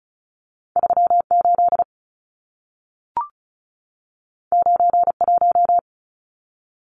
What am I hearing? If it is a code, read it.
Morse code, heard as a keyed tone, about 35 wpm, 700 Hz. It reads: 38 91
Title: Numbers - 2-Digit 35wpm